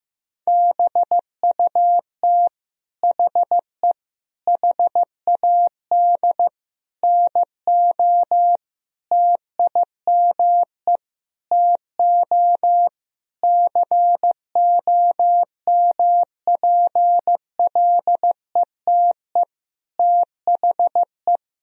Morse code, heard as a keyed tone, about 15 wpm, 700 Hz. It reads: BUT HE HAD NO TIME TO COMPLETE THE
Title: Sentences from Top 1000 Words - Review Part 1 - 15wpm